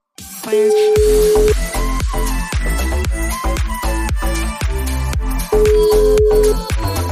Ringback tone (Europe, 425 Hz), with noise.